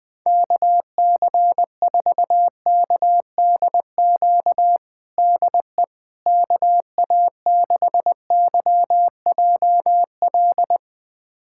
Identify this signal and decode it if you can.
Morse code, heard as a keyed tone, about 20 wpm, 700 Hz. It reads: KC4KDQ DE KA6YJL
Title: USA Callsign DE Callsign - 6 Characters - 20wpm